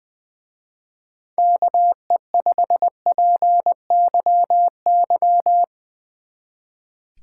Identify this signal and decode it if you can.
Morse code, heard as a keyed tone, about 20 wpm, 700 Hz. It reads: KE5PYY
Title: Speed Racing - USA 2x3 Call Signs 20wpm